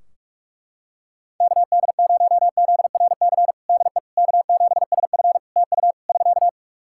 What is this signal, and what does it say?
Morse code, heard as a keyed tone, about 42 wpm, 700 Hz. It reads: KD0ZRC DE K7SF TU <SK>
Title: Practice QSO - Exchange 4 - Ending - 45wpm